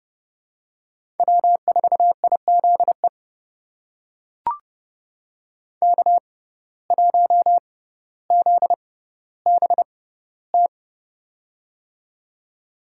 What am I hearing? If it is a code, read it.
Morse code, heard as a keyed tone, about 30 wpm, spaced as for 15 wpm, 700 Hz. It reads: W4IZE K1ZBT
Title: USA Call Signs - 5 Characters - 6x Character Spacing - 30wpm